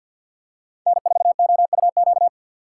Morse code, heard as a keyed tone, about 49 wpm, 690 Hz. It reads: N4OUQ
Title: USA Call Signs - 5 Characters - 3x Character Spacing - 50wpm